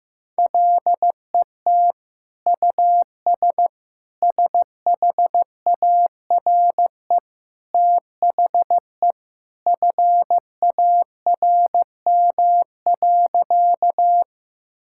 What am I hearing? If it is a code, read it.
Morse code, heard as a keyed tone, about 15 wpm, 700 Hz. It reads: LET US SHARE THE FARM.
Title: Sentences from Top 1000 Words - Review Part 6 - 15wpm